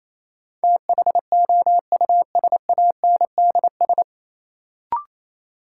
Morse code, heard as a keyed tone, about 28 wpm, 700 Hz. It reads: THOUSANDS
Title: Sets of 4 Words - New Word Drill - 28wpm